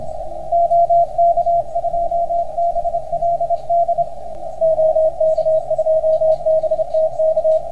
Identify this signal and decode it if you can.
Morse code, heard as a keyed tone, about 27 wpm, 690 Hz, with radio noise. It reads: OK2BLD OK2BTK